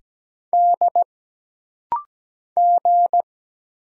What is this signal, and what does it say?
Morse code, heard as a keyed tone, about 17 wpm, 700 Hz. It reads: D G
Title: Single Letters - Rapid-Fire 17wpm